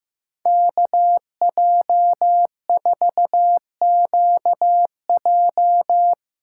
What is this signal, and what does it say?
Morse code, heard as a keyed tone, about 15 wpm, 700 Hz. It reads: KJ4QJ
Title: USA Call Signs - 2x Character Spacing - 15wpm